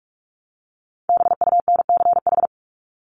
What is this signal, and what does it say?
Morse code, heard as a keyed tone, about 44 wpm, 700 Hz. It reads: BUNCH
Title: Top 1751-2000 Words 45wpm